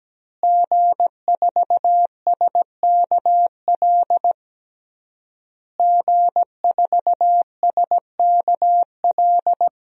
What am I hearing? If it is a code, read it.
Morse code, heard as a keyed tone, about 17 wpm, 700 Hz. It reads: G4SKL G4SKL
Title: DXCC-100 Call Signs - 5 Characters - Repeated 3x - 17wpm